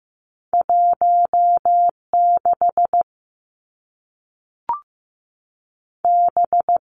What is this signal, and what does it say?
Morse code, heard as a keyed tone, about 15 wpm, 700 Hz. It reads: 1 6 B
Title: Two Challenging Letter-Number Combinations 15wpm